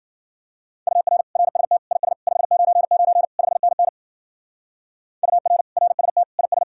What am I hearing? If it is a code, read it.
Morse code, heard as a keyed tone, about 48 wpm, 700 Hz. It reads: UR RST IS 599 5NN UR RST IS
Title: Speed-Racing Common QSO Phrases - 40wpm